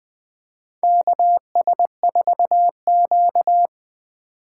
Morse code, heard as a keyed tone, about 20 wpm, 700 Hz. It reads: KS4Q
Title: USA Call Signs - 4 Characters - 4x Character Spacing - 20wpm